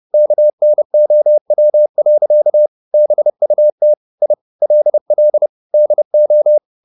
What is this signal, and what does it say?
Morse code, heard as a keyed tone, about 30 wpm, 600 Hz. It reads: KNOW. BUT I LL DO
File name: the old man and the sea / the old man and the sea 1 30wpm